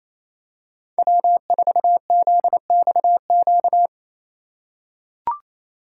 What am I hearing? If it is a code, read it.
Morse code, heard as a keyed tone, about 28 wpm, 700 Hz. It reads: W4ZXQ
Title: USA Call Signs - 5 Characters - 4x Character Spacing - 28wpm